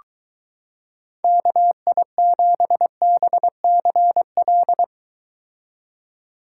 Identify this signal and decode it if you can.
Morse code, heard as a keyed tone, about 23 wpm, 700 Hz. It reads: KI7BCL